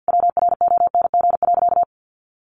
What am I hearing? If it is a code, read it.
Morse code, heard as a keyed tone, about 50 wpm, 700 Hz. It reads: WRONG.
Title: Sentences from Top 1200 Words 50wpm